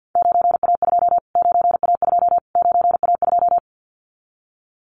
Morse code, heard as a keyed tone, about 50 wpm, 700 Hz. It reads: 9A2 9A2 9A2
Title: Callsign Prefixes for Top 25 DXCC - Repeated 3x - 50wpm